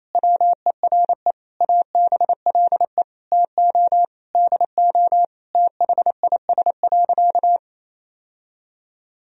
Morse code, heard as a keyed tone, about 27 wpm, 700 Hz. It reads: WERE ABLE TO DO THIS.
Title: Sentences from Top 1000 Words - Review Part 2 - 28wpm